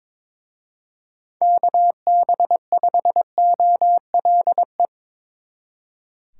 Morse code, heard as a keyed tone, about 24 wpm, 700 Hz. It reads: KB5OLE
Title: Speed Racing - USA 2x3 Call Signs 22wpm